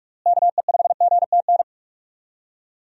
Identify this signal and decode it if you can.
Morse code, heard as a keyed tone, about 46 wpm, 700 Hz. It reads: KE5GTN
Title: USA Call Signs - 6 Characters - Repeated 2x - 45wpm